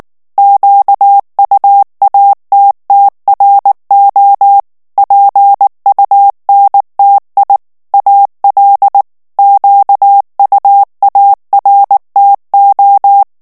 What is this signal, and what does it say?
Morse code, heard as a keyed tone, about 19 wpm, 800 Hz. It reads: QUATTRO PUNTI AL QUARTO
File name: morse mp3 / ITALIEN texte clair 2 vitesse 18 mots minute